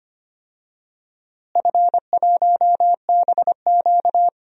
Morse code, heard as a keyed tone, about 25 wpm, 700 Hz. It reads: F1BQ
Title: DXCC-100 Call Signs - 4 Characters - Repeated 2x - 25wpm